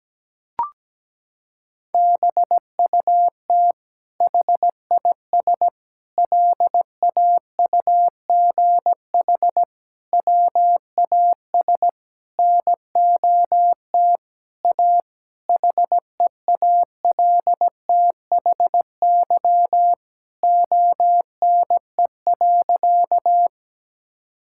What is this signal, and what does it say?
Morse code, heard as a keyed tone, about 17 wpm, 700 Hz. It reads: BUT HIS LAUGH WAS NOT A HEALTHY ONE.
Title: Sentences from Top 1000 Words - Review Part 5 - 17wpm